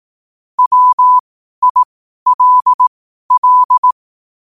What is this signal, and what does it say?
Morse code, heard as a keyed tone, about 18 wpm, spaced as for 12 wpm, 1000 Hz. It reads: WILL
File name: qotdfiles / 12wpm